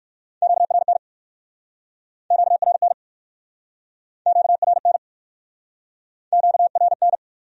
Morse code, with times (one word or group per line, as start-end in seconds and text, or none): 0.4-1.0 QRN
2.3-2.9 QRN
4.3-5.0 QRN
6.3-7.2 QRN